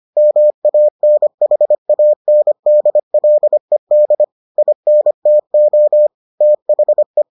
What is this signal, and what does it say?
Morse code, heard as a keyed tone, about 25 wpm, 600 Hz. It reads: MANHANDLED INTO THE